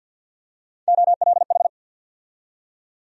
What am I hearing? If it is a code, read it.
Morse code, heard as a keyed tone, about 53 wpm, 700 Hz. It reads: OLH
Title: USA Call Sign Postfixes - 3 Letters 50wpm